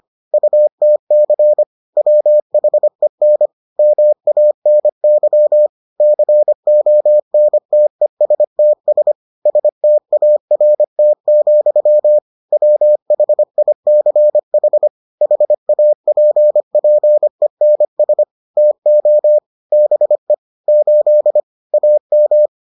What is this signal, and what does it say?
Morse code, heard as a keyed tone, about 25 wpm, 600 Hz. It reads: UTC WHEN MANY CONTESTS START, WHICH HAPPENS TO BE 8 AM